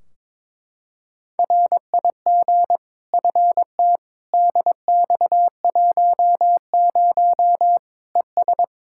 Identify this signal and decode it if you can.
Morse code, heard as a keyed tone, about 22 wpm, 700 Hz. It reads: RIG FT DX10 ES
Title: K1USN SST QSO Practice 22wpm